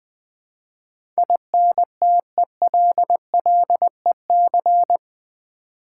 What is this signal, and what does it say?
Morse code, heard as a keyed tone, about 21 wpm, 700 Hz. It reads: INTELLEC
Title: Build A Word - 4 Letters At A Time - 12-Letter Words - 20wpm